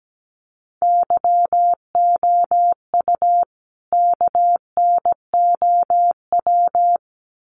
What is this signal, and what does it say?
Morse code, heard as a keyed tone, about 17 wpm, 700 Hz. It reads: YOU KNOW
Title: Sets of 2 Words 17wpm